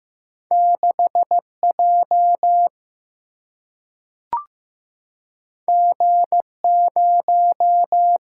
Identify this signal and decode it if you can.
Morse code, heard as a keyed tone, about 15 wpm, 700 Hz. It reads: 6J G0
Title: Single-Letter Single-Number 15wpm